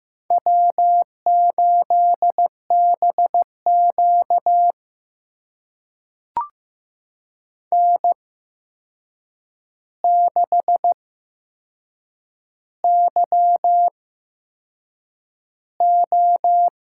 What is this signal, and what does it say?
Morse code, heard as a keyed tone, about 15 wpm, spaced as for 6 wpm, 700 Hz. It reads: W8BQ N6YO
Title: USA Call Signs - 4 Characters - 8x Character Spacing - 15wpm